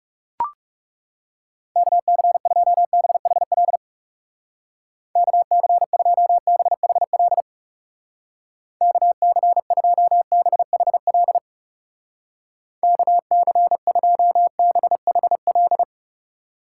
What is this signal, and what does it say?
Morse code, heard as a keyed tone, about 37 wpm, 700 Hz. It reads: KC2BHL KC2BHL KC2BHL KC2BHL
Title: Speed Racing - USA 2x3 Call Signs 30wpm